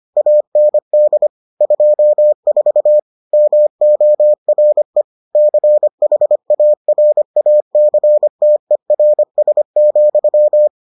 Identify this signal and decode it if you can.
Morse code, heard as a keyed tone, about 25 wpm, 600 Hz. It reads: AND 24 MORE CHARACTERS,